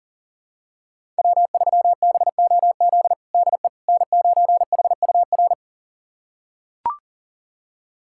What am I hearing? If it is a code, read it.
Morse code, heard as a keyed tone, about 40 wpm, 700 Hz. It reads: W3BOZ DE N9HUR
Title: USA Callsign DE Callsign - 5 Characters - 40wpm